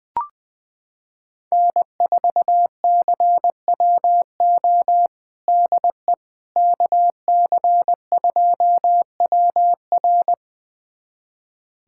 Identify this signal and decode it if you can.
Morse code, heard as a keyed tone, about 21 wpm, 700 Hz. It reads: N4CWO DE KC2WR
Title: USA Callsign DE Callsign - 5 Characters - 20wpm